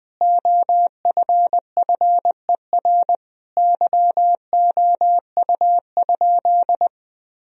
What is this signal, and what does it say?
Morse code, heard as a keyed tone, about 20 wpm, 700 Hz. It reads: OFFER YOU?
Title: Sentences from Top 1000 Words - Review Part 4 - 20wpm